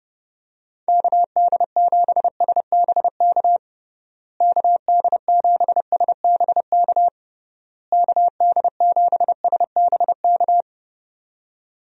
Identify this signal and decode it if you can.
Morse code, heard as a keyed tone, about 30 wpm, 700 Hz. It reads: KD7SBK KD7SBK KD7SBK
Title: USA Call Signs - 6 Characters - Repeated 3x - 30wpm